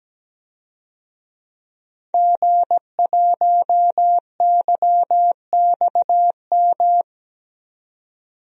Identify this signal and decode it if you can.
Morse code, heard as a keyed tone, about 17 wpm, 700 Hz. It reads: G1YXM
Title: DXCC-100 Call Signs - 5 Characters - 17wpm